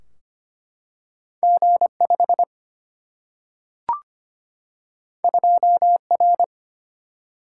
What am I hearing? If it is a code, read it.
Morse code, heard as a keyed tone, about 25 wpm, 700 Hz. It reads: G5 2R
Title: Single-Letter Single-Number 25wpm